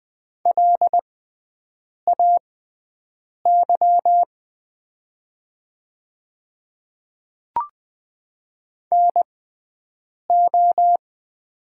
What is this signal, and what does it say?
Morse code, heard as a keyed tone, about 20 wpm, spaced as for 7 wpm, 700 Hz. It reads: LAY NO
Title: 6x Character Spacing - Top 500 Words - 20wpm